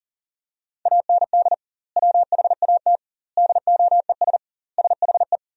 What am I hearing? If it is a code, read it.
Morse code, heard as a keyed tone, about 37 wpm, 700 Hz. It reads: AND WHAT DOES SHE
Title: 2x Word-Spacing - Sentences from Top 400 Words - 40wpm